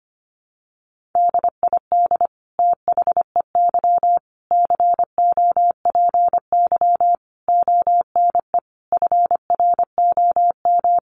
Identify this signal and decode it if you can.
Morse code, heard as a keyed tone, about 25 wpm, 700 Hz. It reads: DID THEY COPY ONE FROM
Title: Sentences from Top 1400 Words 25wpm